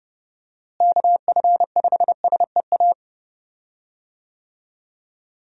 Morse code, heard as a keyed tone, about 30 wpm, 700 Hz. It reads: KF5SEA
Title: USA Call Signs - 6 Characters - Repeated 3x - 30wpm